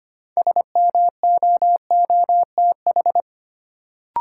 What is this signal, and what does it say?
Morse code, heard as a keyed tone, about 25 wpm, 700 Hz. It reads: SMOOTH
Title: Sentences from Top 1400 Words - New Word Drill - 25wpm